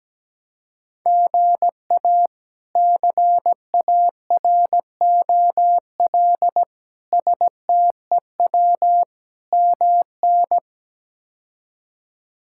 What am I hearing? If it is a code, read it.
Morse code, heard as a keyed tone, about 17 wpm, 700 Hz. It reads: GA CAROL STEW MN
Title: K1USN SST QSO Practice - 17wpm